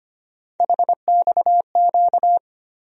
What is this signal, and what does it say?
Morse code, heard as a keyed tone, about 25 wpm, 700 Hz. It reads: HXQ
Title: USA Call Sign Postfixes - 3 Letters 25wpm